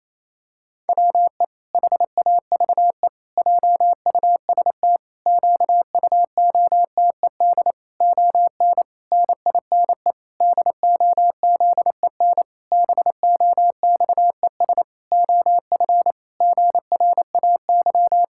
Morse code, heard as a keyed tone, about 28 wpm, 700 Hz. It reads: WE HAVE JUST QUOTED ON NINE DOZEN BOXES OF GRAY
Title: Pangram Challenge v2 - 28wpm